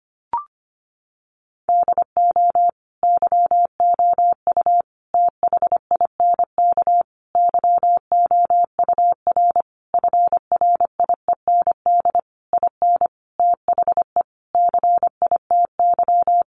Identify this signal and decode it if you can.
Morse code, heard as a keyed tone, about 25 wpm, 700 Hz. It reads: DO YOU THINK YOUR FRIEND IN THE CITY
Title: Sentences from Top 1100 Words 25wpm